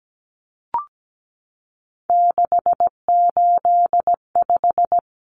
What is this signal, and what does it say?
Morse code, heard as a keyed tone, about 17 wpm, 700 Hz. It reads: 685
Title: Numbers - 3-Digit 17wpm